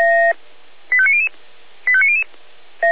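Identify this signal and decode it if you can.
Five-tone selective call (ZVEI), heard as quick runs of five tones, tones of about 70 ms. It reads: 85900 85900
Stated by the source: ZVEI five-tone selective call received on SDR, from the Feuerwehr Orlamünde audio server.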